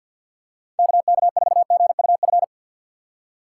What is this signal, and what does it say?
Morse code, heard as a keyed tone, about 50 wpm, 700 Hz. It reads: KK4GUF
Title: USA Call Signs - 6 Characters - 6x Character Spacing - 50wpm